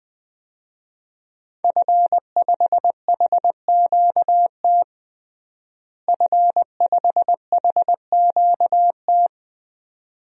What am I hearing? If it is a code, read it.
Morse code, heard as a keyed tone, about 20 wpm, 700 Hz. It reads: F5HQT F5HQT
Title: DXCC-100 Call Signs - 5 Characters - Repeated 2x - 20wpm